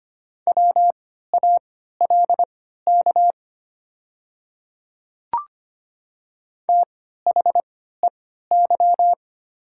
Morse code, heard as a keyed tone, about 25 wpm, spaced as for 14 wpm, 700 Hz. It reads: WALK THEY
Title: 3x Character Spacing - Top 300 Words - 25wpm